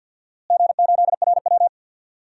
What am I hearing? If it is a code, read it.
Morse code, heard as a keyed tone, about 50 wpm, 700 Hz. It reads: G8RW
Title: DXCC-100 Call Signs - 4 Characters - Repeated 3x - 50wpm